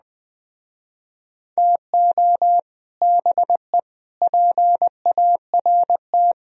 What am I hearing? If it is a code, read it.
Morse code, heard as a keyed tone, about 20 wpm, 700 Hz. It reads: TO BE PART
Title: Speed-Racing - Sets of 4 Words 17wpm